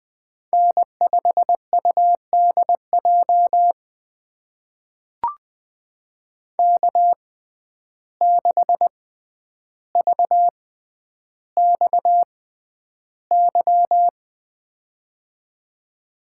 Morse code, heard as a keyed tone, about 20 wpm, spaced as for 10 wpm, 700 Hz. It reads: N5UDJ K6VXY
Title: USA Call Signs - 5 Characters - 6x Character Spacing - 20wpm